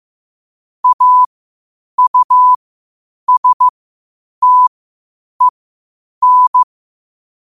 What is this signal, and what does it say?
Morse code, heard as a keyed tone, about 15 wpm, spaced as for 8 wpm, 1000 Hz. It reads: AUSTEN
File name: qotdfiles / 8wpm